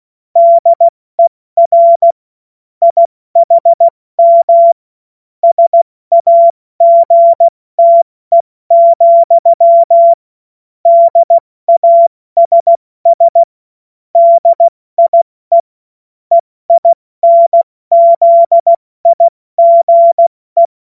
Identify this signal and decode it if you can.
Morse code, heard as a keyed tone, about 16 wpm, 680 Hz. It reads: DER IHM SAGTE, DASS DIE EINZIGE
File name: Morsekurs / Witze / Witz 25. Oktober 2022 70 F80